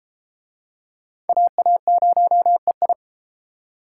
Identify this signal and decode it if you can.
Morse code, heard as a keyed tone, about 32 wpm, 700 Hz. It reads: AA0EI